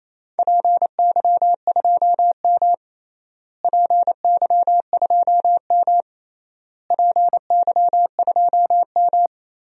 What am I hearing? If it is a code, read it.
Morse code, heard as a keyed tone, about 28 wpm, 700 Hz. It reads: PY2M PY2M PY2M
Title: DXCC-100 Call Signs - 4 Characters - Repeated 3x - 28wpm